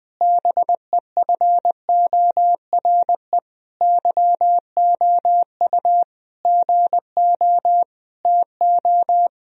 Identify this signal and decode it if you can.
Morse code, heard as a keyed tone, about 21 wpm, 700 Hz. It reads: BEFORE YOU GO TO
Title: Sentences from Top 1000 Words - Review Part 2 - 20wpm